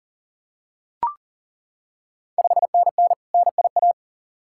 Morse code, heard as a keyed tone, about 40 wpm, 700 Hz. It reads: 5NN NIA
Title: Ontario QSO Party Exchange 40wpm